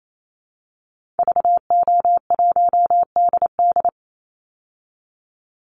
Morse code, heard as a keyed tone, about 28 wpm, 700 Hz. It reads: VO1DD
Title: Call Signs Repeated 3x - 28wpm